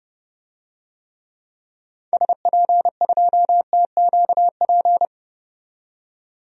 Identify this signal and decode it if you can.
Morse code, heard as a keyed tone, about 31 wpm, 700 Hz. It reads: SP2TQP
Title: DXCC-100 Call Signs - 6 Characters - 30wpm